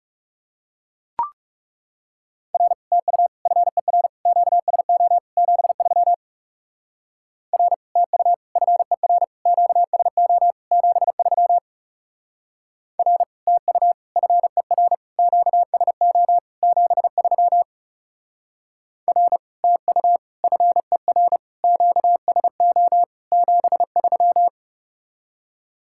Morse code, heard as a keyed tone, about 37 wpm, 700 Hz. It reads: R TU FER QSO 73 R TU FER QSO 73 R TU FER QSO 73 R TU FER QSO 73
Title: Speed-Racing Common QSO Phrases Encore - 30wpm